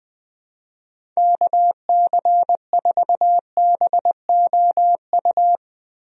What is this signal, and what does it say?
Morse code, heard as a keyed tone, about 20 wpm, 700 Hz. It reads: KC4BOU
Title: USA Call Signs - 6 Characters - Repeated 2x - 20wpm